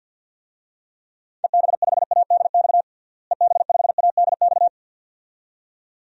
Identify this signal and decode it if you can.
Morse code, heard as a keyed tone, about 53 wpm, 700 Hz. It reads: EB5ADX EB5ADX
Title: DXCC-100 Call Signs - 6 Characters - Repeated 2x - 50wpm